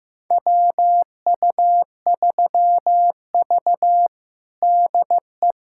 Morse code, heard as a keyed tone, about 15 wpm, 700 Hz. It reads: WU3V DE
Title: USA Callsign DE Callsign - 4 Characters - 15wpm